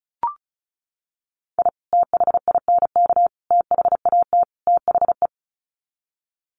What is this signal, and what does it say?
Morse code, heard as a keyed tone, about 34 wpm, 700 Hz. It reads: I THINK THAT THE
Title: Sets of 4 Words 35wpm